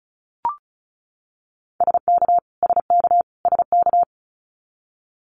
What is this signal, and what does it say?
Morse code, heard as a keyed tone, about 36 wpm, 700 Hz. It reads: SK SK SK
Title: Canadian Provinces and Territories - Repeated 3x - 35wpm